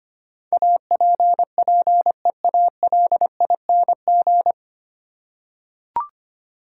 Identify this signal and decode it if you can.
Morse code, heard as a keyed tone, about 25 wpm, 690 Hz. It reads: APPEALING
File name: practice / 25wpm / 9 Letter Words - First 3 Letters Given - 25wpm